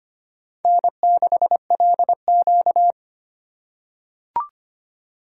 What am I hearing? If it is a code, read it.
Morse code, heard as a keyed tone, about 25 wpm, 700 Hz. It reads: N6LQ
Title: USA Call Signs - 4 Characters - 4x Character Spacing - 25wpm